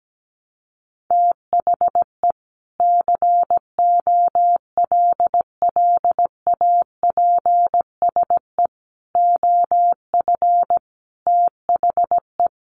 Morse code, heard as a keyed tone, about 17 wpm, 700 Hz. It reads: THE COLLAPSE OF THE
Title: Sets of 5 Words Encore 17wpm